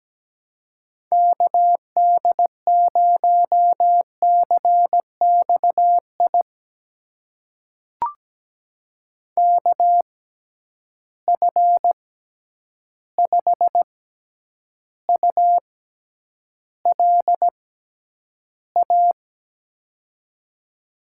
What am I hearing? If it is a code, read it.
Morse code, heard as a keyed tone, about 17 wpm, spaced as for 9 wpm, 700 Hz. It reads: KD0CXI KF5ULA
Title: USA Call Signs - 6 Characters - 6x Character Spacing - 17wpm